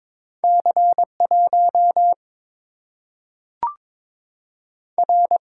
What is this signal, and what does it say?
Morse code, heard as a keyed tone, about 22 wpm, 700 Hz. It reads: C 1 R
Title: Single-Letter Single-Number 22wpm